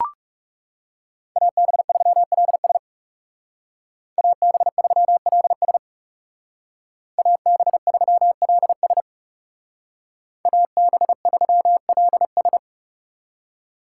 Morse code, heard as a keyed tone, about 37 wpm, 700 Hz. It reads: AB3LS AB3LS AB3LS AB3LS
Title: Speed Racing - USA 2x2 Call Signs 30wpm